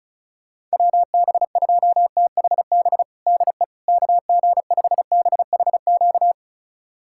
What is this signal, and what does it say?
Morse code, heard as a keyed tone, about 36 wpm, 690 Hz. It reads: WB2THB DE KG5BHQ
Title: USA Callsign DE Callsign 35wpm